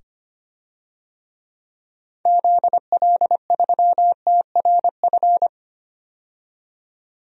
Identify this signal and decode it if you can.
Morse code, heard as a keyed tone, about 25 wpm, 700 Hz. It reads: ZL3TRF
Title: DXCC-100 Call Signs - 6 Characters - 25wpm